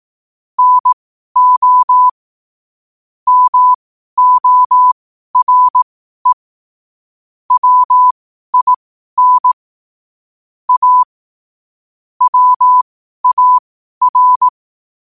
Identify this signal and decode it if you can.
Morse code, heard as a keyed tone, about 18 wpm, spaced as for 12 wpm, 1000 Hz. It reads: NO MORE WIN A WAR